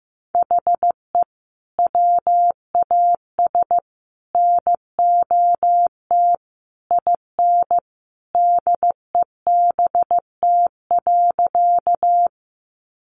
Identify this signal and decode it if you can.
Morse code, heard as a keyed tone, about 15 wpm, 700 Hz. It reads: HE WAS NOT IN DEBT.
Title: Sentences from Top 1300 Words 15wpm